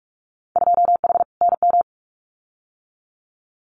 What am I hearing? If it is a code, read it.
Morse code, heard as a keyed tone, about 46 wpm, 700 Hz. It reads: 2H NM
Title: Winter Field Day Exchanges Encore 45wpm